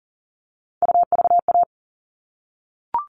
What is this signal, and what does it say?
Morse code, heard as a keyed tone, about 40 wpm, 700 Hz. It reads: UVA
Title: All Cut Numbers - 3-Digit 40wpm